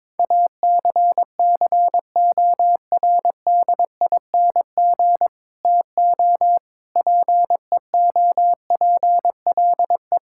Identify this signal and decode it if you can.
Morse code, heard as a keyed tone, about 21 wpm, 700 Hz. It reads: ACCORDING TO PEOPLE
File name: practice / 22wpm / Sets of 7 Words - 22wpm